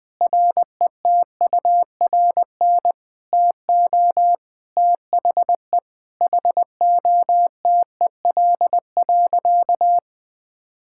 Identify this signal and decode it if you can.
Morse code, heard as a keyed tone, about 20 wpm, 700 Hz. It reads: RETURN TO THE HOTEL.
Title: Sentences from Top 1600 Words 20wpm